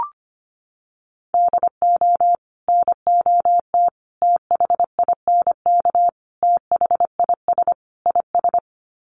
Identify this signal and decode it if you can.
Morse code, heard as a keyed tone, about 25 wpm, 700 Hz. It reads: DO NOT THINK THIS IS
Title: Sets of 5 Words Encore2 25wpm